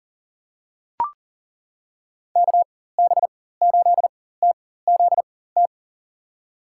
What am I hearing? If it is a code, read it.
Morse code, heard as a keyed tone, about 37 wpm, spaced as for 18 wpm, 700 Hz. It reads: KB8TZT
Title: USA Call Signs - 4x Character Spacing - 40wpm